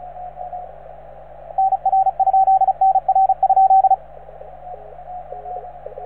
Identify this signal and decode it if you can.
Morse code, heard as a keyed tone, about 34 wpm, 770 Hz, with radio noise. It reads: NR? NR?